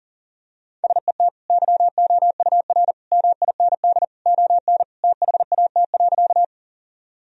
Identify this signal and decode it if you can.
Morse code, heard as a keyed tone, about 40 wpm, 700 Hz. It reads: SET YOUR MIND ON THAT.
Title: Sentences from Top 1000 Words - Review Part 1 - 40wpm